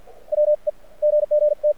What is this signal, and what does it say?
Morse code, heard as a keyed tone, about 53 wpm, spaced as for 45 wpm, 600 Hz. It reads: WE GG T
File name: archived / CRSnet